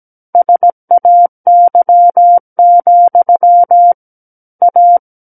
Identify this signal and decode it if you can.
Morse code, heard as a keyed tone, about 17 wpm, 700 Hz. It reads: SAY, A
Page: SKCC Learning Center - Morse Code Practice Files